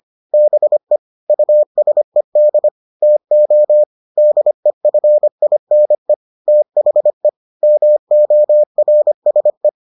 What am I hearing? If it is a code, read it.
Morse code, heard as a keyed tone, about 25 wpm, 600 Hz. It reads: BE USED TO DEFINE THE MORSE